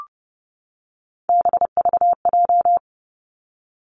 Morse code, heard as a keyed tone, about 30 wpm, 700 Hz. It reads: BVJ
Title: Three Challenging Letter-Number Combinations 30wpm